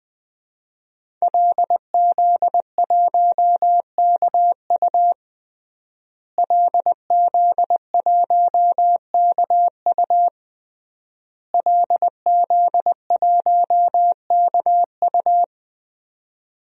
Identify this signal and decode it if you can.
Morse code, heard as a keyed tone, about 20 wpm, 700 Hz. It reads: LZ1KU LZ1KU LZ1KU
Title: DXCC-100 Call Signs - 5 Characters - Repeated 3x - 20wpm